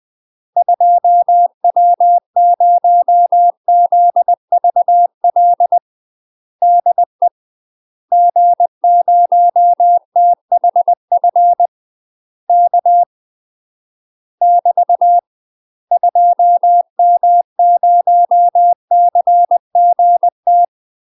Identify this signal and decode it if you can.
Morse code, heard as a keyed tone, about 20 wpm, 700 Hz. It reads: 2W0ZVL DE G0THF K <BT> 2M0CGT
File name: UK MASTER QSOs TRAINING / 20WPM / UK MASTER QSOs 0008